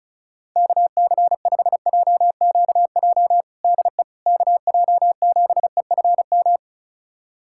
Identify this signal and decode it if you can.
Morse code, heard as a keyed tone, about 36 wpm, 700 Hz. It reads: KC5JQJ DE KJ7EFM
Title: USA Callsign DE Callsign - 6 Characters - 35wpm